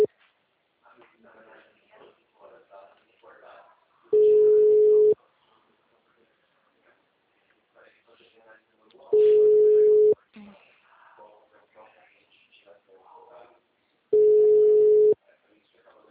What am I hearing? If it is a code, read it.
Ringback tone (Europe, 425 Hz), with speech.